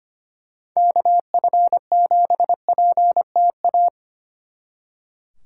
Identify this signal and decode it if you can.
Morse code, heard as a keyed tone, about 29 wpm, 700 Hz. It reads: KF7PTA
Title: Speed Racing - USA 2x3 Call Signs 25wpm